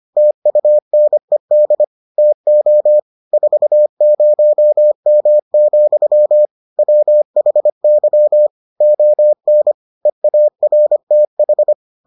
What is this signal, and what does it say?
Morse code, heard as a keyed tone, about 25 wpm, 600 Hz. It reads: TUNED TO 40M, WHY ON EARTH